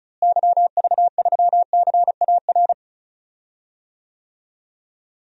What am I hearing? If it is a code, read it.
Morse code, heard as a keyed tone, about 36 wpm, 700 Hz. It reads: YV3CAR
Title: DXCC-100 Call Signs - 6 Characters - Repeated 2x - 35wpm